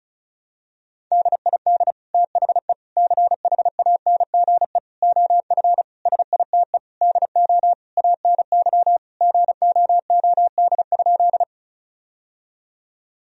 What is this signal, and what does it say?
Morse code, heard as a keyed tone, about 36 wpm, 700 Hz. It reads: DID THE CHANGE OF SITE DO ANY GOOD?
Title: Sentences from Top 1000 Words - Review Part 4 - 35wpm